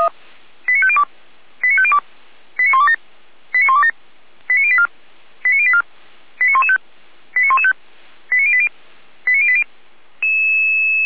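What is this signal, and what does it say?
Five-tone selective call (ZVEI), heard as quick runs of five tones, tones of about 70 ms. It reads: 89502 89502 89127 89127 89084 89084 89115 89115 89080 89080
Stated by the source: ZVEI five-tone call received by SDR.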